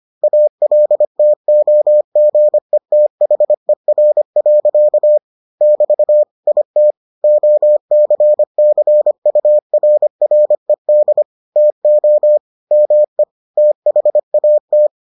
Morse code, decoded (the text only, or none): ALTOGETHER. <BT> IT OCCURRED TO ME THAT